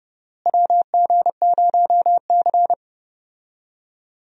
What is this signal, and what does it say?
Morse code, heard as a keyed tone, about 30 wpm, 700 Hz. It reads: WG0C
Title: USA Call Signs - 4 Characters - Repeated 2x - 30wpm